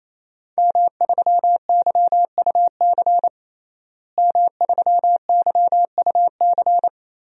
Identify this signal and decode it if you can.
Morse code, heard as a keyed tone, about 28 wpm, 700 Hz. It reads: M3YUC M3YUC
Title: DXCC-100 Call Signs - 5 Characters - Repeated 3x - 28wpm